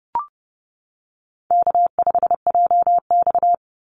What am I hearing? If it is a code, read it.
Morse code, heard as a keyed tone, about 30 wpm, 700 Hz. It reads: K5JX
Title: Easy Call Signs - 30wpm